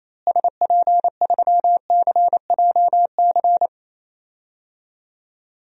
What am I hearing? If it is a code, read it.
Morse code, heard as a keyed tone, about 28 wpm, 700 Hz. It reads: SP3CJC
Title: DXCC-100 Call Signs - 6 Characters - Repeated 3x - 28wpm